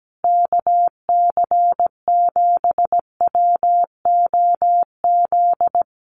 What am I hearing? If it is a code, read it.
Morse code, heard as a keyed tone, about 17 wpm, 700 Hz. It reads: KC7WOZ
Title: Common QSO Phrases Encore 17wpm